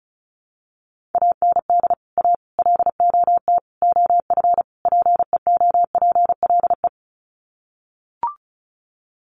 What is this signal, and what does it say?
Morse code, heard as a keyed tone, about 34 wpm, 700 Hz. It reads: AND A LOT OF PEOPLE
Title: Sets of 5 Words 35wpm